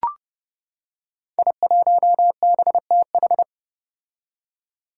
Morse code, heard as a keyed tone, about 30 wpm, 700 Hz. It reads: I1BTH
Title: Five Challenging Letter-Number Combinations 30wpm